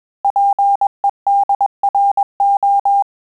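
Morse code, heard as a keyed tone, about 21 wpm, 800 Hz. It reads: PEDRO
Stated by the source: Morse QSO practice at 20 words per minute, in Italian and English.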